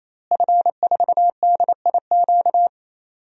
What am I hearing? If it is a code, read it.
Morse code, heard as a keyed tone, about 28 wpm, 700 Hz. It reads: F4DIQ
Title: DXCC-100 Call Signs - 5 Characters - Repeated 3x - 28wpm